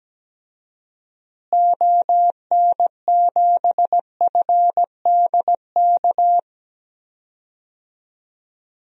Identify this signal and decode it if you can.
Morse code, heard as a keyed tone, about 17 wpm, 700 Hz. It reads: ON7FDK
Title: DXCC-100 Call Signs - 6 Characters - Repeated 3x - 17wpm